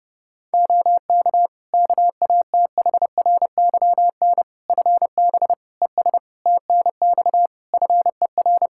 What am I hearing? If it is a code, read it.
Morse code, heard as a keyed tone, about 30 wpm, 700 Hz. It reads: OK KATHRYN FB ES TNX FER
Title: K1USN SST QSO Practice 30wpm